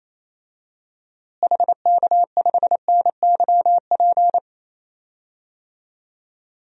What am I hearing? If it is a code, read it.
Morse code, heard as a keyed tone, about 28 wpm, 700 Hz. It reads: HK5NYP
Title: DXCC-100 Call Signs - 6 Characters - Repeated 2x - 28wpm